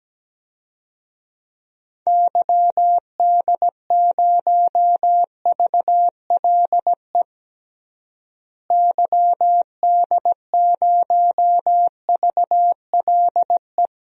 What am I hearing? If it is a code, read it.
Morse code, heard as a keyed tone, about 17 wpm, 700 Hz. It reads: YD0VLE YD0VLE
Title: DXCC-100 Call Signs - 6 Characters - Repeated 3x - 17wpm